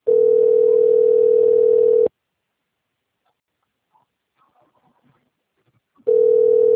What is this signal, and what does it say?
Ringback tone (North America), 440 + 480 Hz.